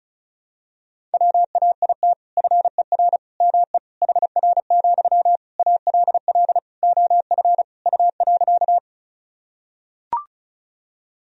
Morse code, heard as a keyed tone, about 36 wpm, 700 Hz. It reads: WAIT FER ME HR, ALL OF U.
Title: Sentences from Top 500 Words and Abbreviations 35wpm